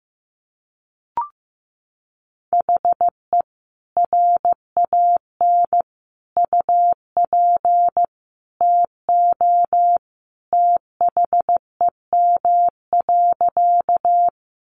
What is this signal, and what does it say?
Morse code, heard as a keyed tone, about 15 wpm, 700 Hz. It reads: HE RAN UP TO THEM.
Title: Sentences from Top 500 Words 15wpm